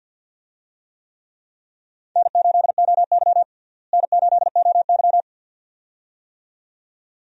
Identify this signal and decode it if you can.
Morse code, heard as a keyed tone, about 53 wpm, 700 Hz. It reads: N8OY N8OY
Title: USA Call Signs - 4 Characters - Repeated 2x - 50wpm